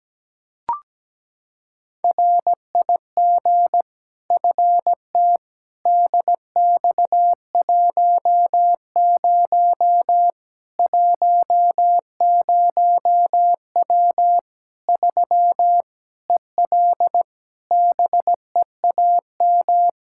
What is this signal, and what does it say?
Morse code, heard as a keyed tone, about 17 wpm, 700 Hz. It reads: RIG FT DX10 10W 3 EL BEAM
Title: K1USN SST QSO Practice 17wpm